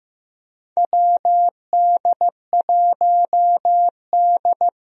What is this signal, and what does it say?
Morse code, heard as a keyed tone, about 16 wpm, 700 Hz. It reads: WD1D
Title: USA Call Signs - 4 Characters - Repeated 3x - 15wpm